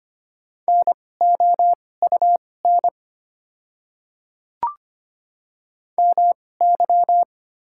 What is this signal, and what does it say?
Morse code, heard as a keyed tone, about 25 wpm, spaced as for 18 wpm, 700 Hz. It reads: NOUN MY
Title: 2x Character Spacing - Top 500 Words - 25wpm